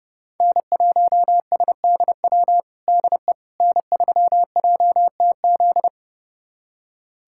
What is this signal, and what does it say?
Morse code, heard as a keyed tone, about 30 wpm, 700 Hz. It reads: N1SDW DE N3JTZ
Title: USA Callsign DE Callsign - 5 Characters - 30wpm